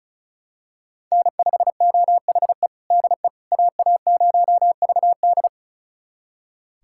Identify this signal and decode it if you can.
Morse code, heard as a keyed tone, about 36 wpm, 700 Hz. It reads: N5OHE DE AA0VD
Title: USA Callsign DE Callsign - 5 Characters - 35wpm